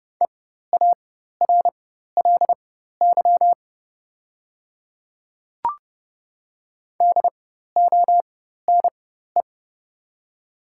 Morse code, heard as a keyed tone, about 31 wpm, spaced as for 14 wpm, 700 Hz. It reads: EARLY DONE
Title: 4x Character Spacing - Top 500 Words - 30wpm